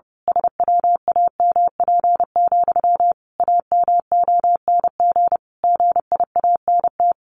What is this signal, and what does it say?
Morse code, heard as a keyed tone, about 30 wpm, 700 Hz. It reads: SWAMP, AMONG GIANT
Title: A Sound Of Thunder 30wpm